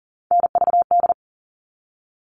Morse code, heard as a keyed tone, about 40 wpm, 700 Hz. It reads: NVD
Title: All Cut Numbers - 3-Digit 40wpm